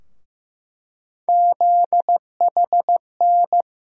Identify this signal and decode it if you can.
Morse code, heard as a keyed tone, about 15 wpm, 700 Hz. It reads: ZHN
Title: USA Call Sign Postfixes - 3 Letters 15wpm